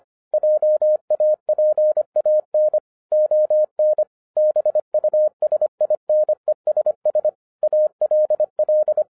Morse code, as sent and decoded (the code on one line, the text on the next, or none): .--- .- .--. .- -. / --- -. / -... ..- ... .. -. ... ... / .- .-.. .-..
JAPAN ON BUSINESS ALL